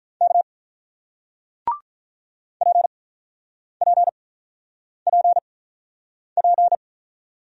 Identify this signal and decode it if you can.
Morse code, heard as a keyed tone, about 47 wpm, spaced as for 9 wpm, 700 Hz. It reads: K PPPP